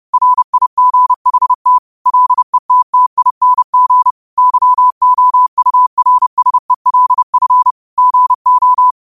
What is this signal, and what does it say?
Morse code, heard as a keyed tone, about 30 wpm, 1000 Hz. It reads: RIGHT LETTING YOURSELF GO
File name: qotdfiles / 30wpm